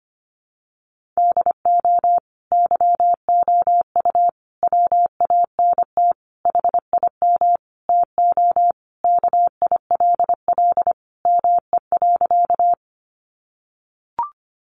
Morse code, heard as a keyed tone, about 25 wpm, 700 Hz. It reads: DO YOU WANT HIM TO KILL ME.
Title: Sentences from Top 1400 Words 25wpm